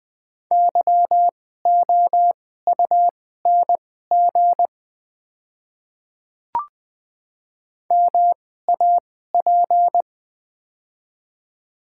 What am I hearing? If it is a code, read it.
Morse code, heard as a keyed tone, about 20 wpm, spaced as for 14 wpm, 700 Hz. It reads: YOUNG MAP
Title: 2x Character Spacing - Top 500 Words - 20wpm